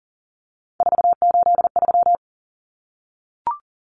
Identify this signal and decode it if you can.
Morse code, heard as a keyed tone, about 40 wpm, 700 Hz. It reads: 483
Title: Numbers - 3-Digit 40wpm